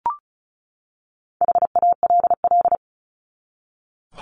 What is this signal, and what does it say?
Morse code, heard as a keyed tone, about 34 wpm, 700 Hz. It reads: HALL
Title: Top 1751-2000 Words 35wpm